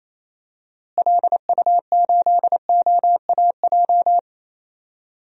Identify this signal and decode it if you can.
Morse code, heard as a keyed tone, about 28 wpm, 700 Hz. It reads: LU8OAJ
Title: DXCC-100 Call Signs - 6 Characters - Repeated 2x - 28wpm